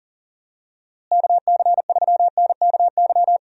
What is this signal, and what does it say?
Morse code, heard as a keyed tone, about 40 wpm, 700 Hz. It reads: KC3NKY
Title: USA Call Signs - 6 Characters - Repeated 3x - 40wpm